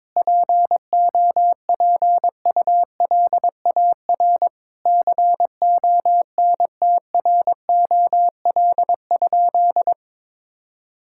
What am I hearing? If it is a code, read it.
Morse code, heard as a keyed tone, about 21 wpm, 700 Hz. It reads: POPULAR CONTROL?
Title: Sentences from Top 1000 Words - Review Part 5 - 22wpm